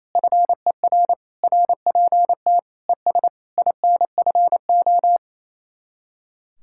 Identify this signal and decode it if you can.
Morse code, heard as a keyed tone, about 35 wpm, 700 Hz. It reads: FER RPT ES INFO